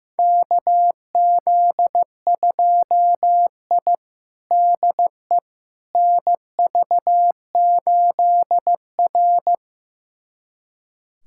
Morse code, heard as a keyed tone, about 15 wpm, 700 Hz. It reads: KZ2I DE NV8R
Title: USA Callsign DE Callsign - 4 Characters - 15wpm